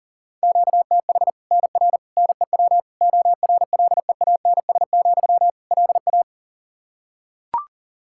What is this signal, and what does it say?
Morse code, heard as a keyed tone, about 37 wpm, 700 Hz. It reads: QTH NR NEW ORLEANS, LA
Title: K1USN SST QSO Practice 40wpm